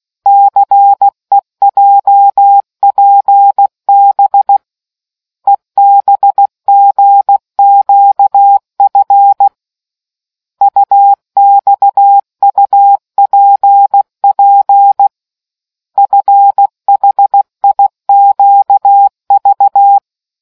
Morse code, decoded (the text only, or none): CEJPB EBGQF UXUPP FHIQV